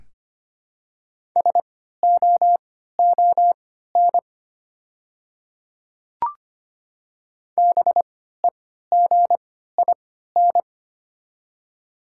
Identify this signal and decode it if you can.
Morse code, heard as a keyed tone, about 25 wpm, spaced as for 14 wpm, 690 Hz. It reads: SOON BEGIN
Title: 3x Character Spacing - Top 500 Words - 25wpm